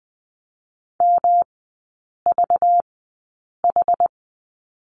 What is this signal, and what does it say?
Morse code, heard as a keyed tone, about 20 wpm, spaced as for 8 wpm, 690 Hz. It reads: MVH
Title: 191 Lesson 31e - Letter v - Three Characters 20wpm